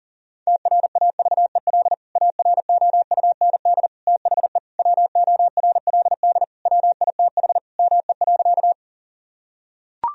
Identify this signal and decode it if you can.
Morse code, heard as a keyed tone, about 40 wpm, 700 Hz. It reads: TRAVEL AROUND THE WORLD WITH ME.
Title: Sentences from Top 1000 Words - Review Part 3 - 40wpm